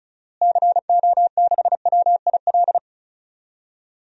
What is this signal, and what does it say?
Morse code, heard as a keyed tone, about 36 wpm, 700 Hz. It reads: CO6WIL
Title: DXCC-100 Call Signs - 6 Characters - Repeated 2x - 35wpm